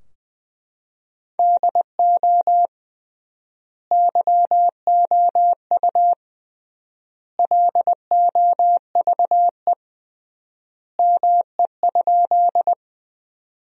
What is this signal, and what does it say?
Morse code, heard as a keyed tone, about 20 wpm, 700 Hz. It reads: DO YOU LOVE ME?
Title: 3x Word-Spacing - Sentences from Top 400 Words - 20wpm